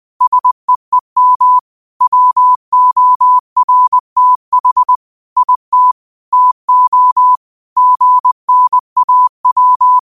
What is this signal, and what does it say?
Morse code, heard as a keyed tone, about 20 wpm, 1000 Hz. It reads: SEEM WORTH IT TO GNAW